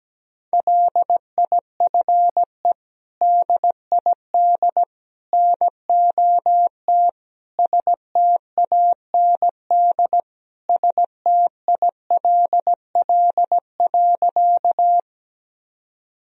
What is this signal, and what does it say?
Morse code, heard as a keyed tone, about 17 wpm, 700 Hz. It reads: LIFE DID NOT STAND STILL.
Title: Sentences from Top 1000 Words - Review Part 1 - 17wpm